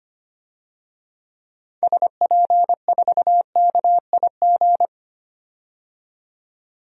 Morse code, heard as a keyed tone, about 25 wpm, 700 Hz. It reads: SP4KIG
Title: DXCC-100 Call Signs - 6 Characters - 25wpm